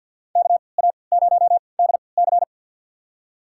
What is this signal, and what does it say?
Morse code, heard as a keyed tone, about 48 wpm, spaced as for 28 wpm, 700 Hz. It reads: KA0DC
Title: USA Call Signs - 3x Character Spacing - 50wpm